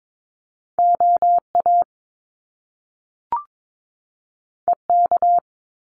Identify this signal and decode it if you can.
Morse code, heard as a keyed tone, about 22 wpm, 700 Hz. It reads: OA EK
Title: Top Two Letter Combinations in Words 22wpm